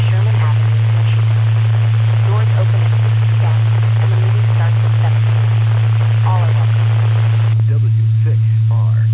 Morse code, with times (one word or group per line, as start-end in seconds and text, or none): none